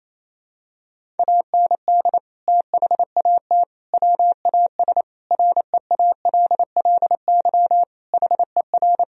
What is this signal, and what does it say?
Morse code, heard as a keyed tone, about 28 wpm, 700 Hz. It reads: AND THAT WAS REALLY HER
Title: Sentences from Top 1000 Words - Review Part 3 - 28wpm